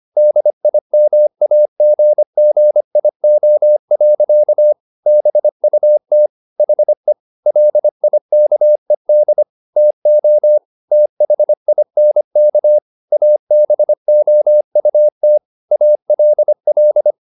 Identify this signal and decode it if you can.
Morse code, heard as a keyed tone, about 25 wpm, 600 Hz. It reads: DIMAGGIO. BUT HE LIKED TO THINK ABOUT ALL